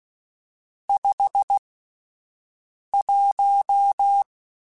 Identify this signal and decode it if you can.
Morse code, heard as a keyed tone, about 16 wpm, spaced as for 5 wpm, 800 Hz. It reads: 51
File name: Spaced Characters / Spaced Characters-mp3s / Weather4 21 sp @10